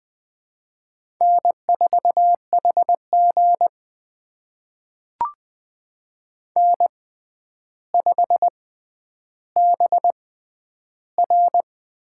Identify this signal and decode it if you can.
Morse code, heard as a keyed tone, about 20 wpm, spaced as for 10 wpm, 700 Hz. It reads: N4HG N5BR
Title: USA Call Signs - 6x Character Spacing - 20wpm